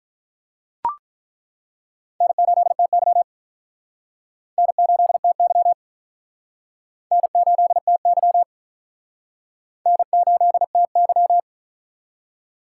Morse code, heard as a keyed tone, about 47 wpm, 700 Hz. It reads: N8TY N8TY N8TY N8TY